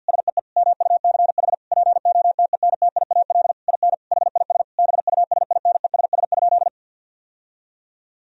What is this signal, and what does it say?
Morse code, heard as a keyed tone, about 50 wpm, 700 Hz. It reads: SEE MUCH POTENTIAL IN HIS BUSINESS?
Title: Sentences from Top 1000 Words - Review Part 4 - 50wpm